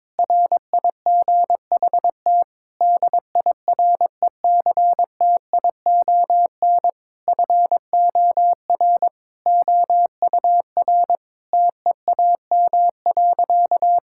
Morse code, heard as a keyed tone, about 22 wpm, 700 Hz. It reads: RIGHT DIRECTION FOR OUR TEAM.
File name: practice / 22wpm / Sentences with 8-Word Phrases - 22wpm